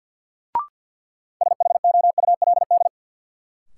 Morse code, heard as a keyed tone, about 53 wpm, 700 Hz. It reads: SHOULD
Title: Sentences from Top 300 Words - New Word Drill - 50wpm